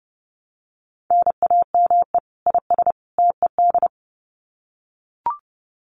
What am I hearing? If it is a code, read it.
Morse code, heard as a keyed tone, about 30 wpm, 700 Hz. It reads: NAME IS TED
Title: Common QSO Phrases 30wpm